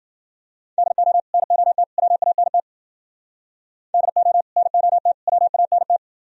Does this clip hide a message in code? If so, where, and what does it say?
Yes: Morse code from 0.8 to 6.0 s, heard as a keyed tone, about 53 wpm, 700 Hz. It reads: DO NOT WANT DO NOT WANT